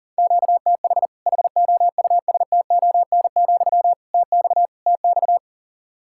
Morse code, heard as a keyed tone, about 40 wpm, 700 Hz. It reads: QTH HOUSTON, TX TX
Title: Practice QSO - Exchange 1 - Beginning - 40wpm